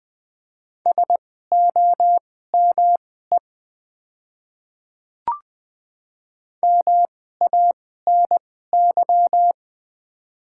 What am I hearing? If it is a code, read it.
Morse code, heard as a keyed tone, about 21 wpm, spaced as for 15 wpm, 700 Hz. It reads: SOME MANY
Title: 2x Character Spacing - Top 100 Words - 20wpm